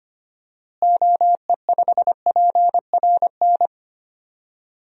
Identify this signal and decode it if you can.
Morse code, heard as a keyed tone, about 25 wpm, 700 Hz. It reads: OE5PRN
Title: DXCC-100 Call Signs - 6 Characters - Repeated 3x - 25wpm